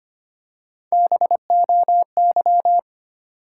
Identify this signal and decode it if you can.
Morse code, heard as a keyed tone, about 25 wpm, 700 Hz. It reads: BOY
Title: Three Letter Words - 25wpm